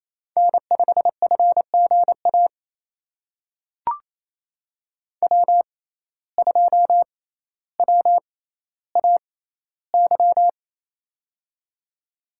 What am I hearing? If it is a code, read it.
Morse code, heard as a keyed tone, about 28 wpm, spaced as for 14 wpm, 700 Hz. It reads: N5FGA W2WAY
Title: USA Call Signs - 5 Characters - 6x Character Spacing - 28wpm